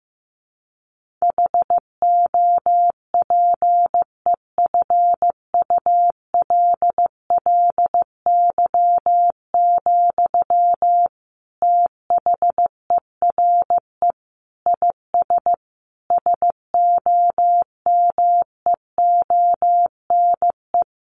Morse code, heard as a keyed tone, about 14 wpm, 700 Hz. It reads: HOPEFULLY, THERE IS SOMEONE
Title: Sentences from Top 1500 Words 15wpm